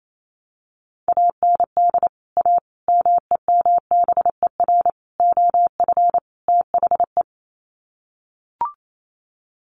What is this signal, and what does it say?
Morse code, heard as a keyed tone, about 28 wpm, 700 Hz. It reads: AND A MEMBER OF THE
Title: Sets of 5 Words 28wpm